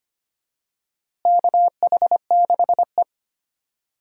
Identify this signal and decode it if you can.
Morse code, heard as a keyed tone, about 25 wpm, 700 Hz. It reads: KH6E
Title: DXCC-100 Call Signs - 4 Characters - Repeated 2x - 25wpm